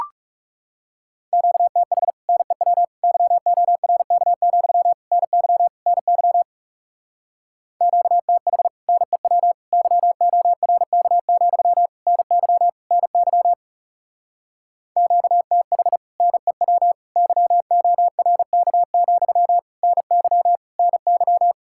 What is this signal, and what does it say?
Morse code, heard as a keyed tone, about 37 wpm, 700 Hz. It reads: QTH NEW YORK, NY NY QTH NEW YORK, NY NY QTH NEW YORK, NY NY
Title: Speed-Racing Common QSO Phrases - 30wpm